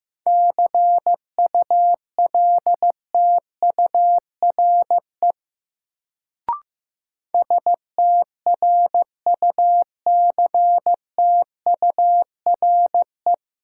Morse code, heard as a keyed tone, about 15 wpm, 700 Hz. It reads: CULTURE STRUCTURE
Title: Sentences from Top 1000 Words - Review Part 6 - New Word Drill - 15wpm